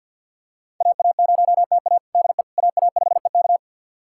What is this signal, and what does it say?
Morse code, heard as a keyed tone, about 49 wpm, 700 Hz. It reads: AA0TA DE AA5EK
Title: USA Callsign DE Callsign - 5 Characters - 50wpm